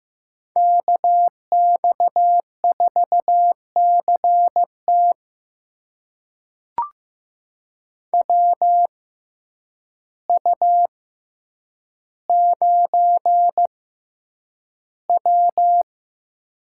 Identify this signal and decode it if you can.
Morse code, heard as a keyed tone, about 15 wpm, spaced as for 8 wpm, 700 Hz. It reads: KX4CT WU9W
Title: USA Call Signs - 6x Character Spacing - 15wpm